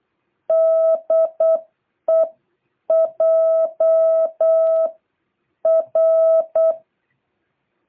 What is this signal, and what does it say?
Morse code, heard as a keyed tone, about 8 wpm, spaced as for 7 wpm, 660 Hz. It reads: DE JR